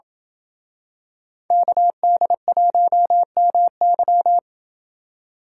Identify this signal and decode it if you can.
Morse code, heard as a keyed tone, about 27 wpm, 700 Hz. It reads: KD1MY